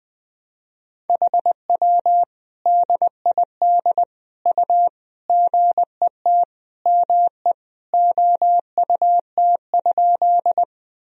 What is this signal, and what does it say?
Morse code, heard as a keyed tone, about 21 wpm, 700 Hz. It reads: HW DID U GET ME OUT?
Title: Sentences from Top 200 Words and Abbreviations 20wpm